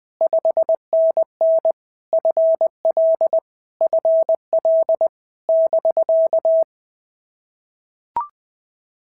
Morse code, heard as a keyed tone, about 20 wpm, 650 Hz. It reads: 5NN FL FL <BK>
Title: POTA QSO Practice - 20wpm